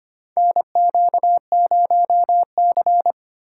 Morse code, heard as a keyed tone, about 25 wpm, 700 Hz. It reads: NQ0C
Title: USA Call Signs - 4 Characters - 2x Character Spacing - 25wpm